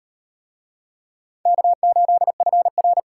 Morse code, heard as a keyed tone, about 38 wpm, 700 Hz. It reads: K8FR